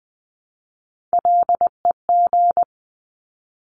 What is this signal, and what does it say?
Morse code, heard as a keyed tone, about 20 wpm, 700 Hz. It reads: LEG
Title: 260 Lesson 42e - Words 20-20wpm